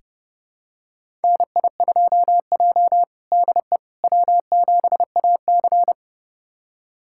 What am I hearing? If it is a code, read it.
Morse code, heard as a keyed tone, about 30 wpm, 700 Hz. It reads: NI2J DE W7AC
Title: USA Callsign DE Callsign - 4 Characters - 30wpm